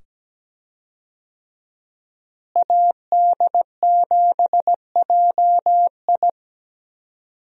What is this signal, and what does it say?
Morse code, heard as a keyed tone, about 17 wpm, 700 Hz. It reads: AD7JI
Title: USA Call Signs - 5 Characters - Repeated 2x - 17wpm